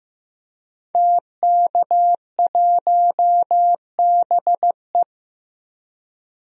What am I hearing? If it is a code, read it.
Morse code, heard as a keyed tone, about 15 wpm, 700 Hz. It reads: TK1BE
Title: DXCC-100 Call Signs - 5 Characters - 15wpm